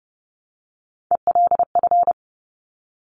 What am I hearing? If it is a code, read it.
Morse code, heard as a keyed tone, about 30 wpm, 700 Hz. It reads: ELF
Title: Top Three Letter Combinations in Words 30wpm